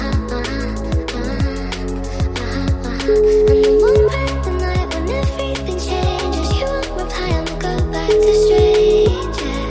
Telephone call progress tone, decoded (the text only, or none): ringback tone